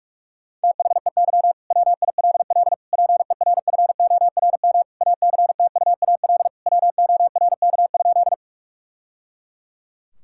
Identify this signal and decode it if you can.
Morse code, heard as a keyed tone, about 46 wpm, 700 Hz. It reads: THEY WILL PERFORM ACTUAL WORK?
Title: Sentences from Top 1000 Words - Review Part 5 - 45wpm